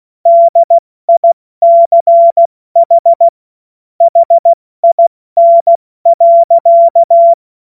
Morse code, heard as a keyed tone, about 16 wpm, 680 Hz. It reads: DICH HIN.
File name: Morsekurs / Witze / Witz 24. August 2021, 70 F80